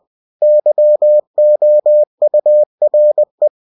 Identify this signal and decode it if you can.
Morse code, heard as a keyed tone, about 20 wpm, 600 Hz. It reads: YOURE
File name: cw-books / winnie-the-pooh / winnie-20wpm-page-0017